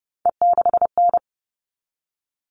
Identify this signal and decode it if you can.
Morse code, heard as a keyed tone, about 30 wpm, 700 Hz. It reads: E6N
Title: All Cut Numbers - 3-Digit 30wpm